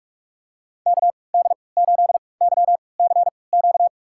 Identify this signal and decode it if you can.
Morse code, heard as a keyed tone, about 46 wpm, spaced as for 26 wpm, 700 Hz. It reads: KD8YCQ
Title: USA Call Signs - 6 Characters - 3x Character Spacing - 45wpm